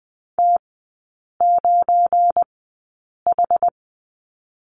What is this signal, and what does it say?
Morse code, heard as a keyed tone, about 20 wpm, spaced as for 8 wpm, 690 Hz. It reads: T9H
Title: 172 Lesson 28e - Number 9 - Three Characters 20wpm